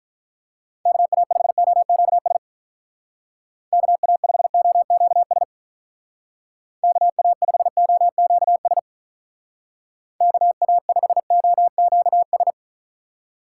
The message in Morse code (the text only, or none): KA5OQS KA5OQS KA5OQS KA5OQS